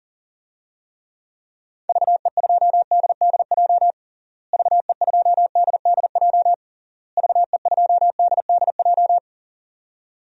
Morse code, heard as a keyed tone, about 40 wpm, 700 Hz. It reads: VE2DDJ VE2DDJ VE2DDJ
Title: DXCC-100 Call Signs - 6 Characters - Repeated 3x - 40wpm